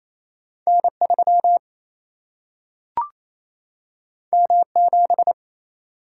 Morse code, heard as a keyed tone, about 28 wpm, 700 Hz. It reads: N 3 M 7
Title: Single-Letter Single-Number 28wpm